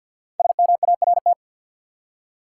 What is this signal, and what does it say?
Morse code, heard as a keyed tone, about 49 wpm, 700 Hz. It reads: SMART
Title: Five Letter Words - 50wpm